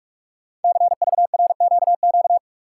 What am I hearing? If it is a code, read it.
Morse code, heard as a keyed tone, about 44 wpm, 700 Hz. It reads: CVRQQ
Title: Code Groups - 5 Letters - 45wpm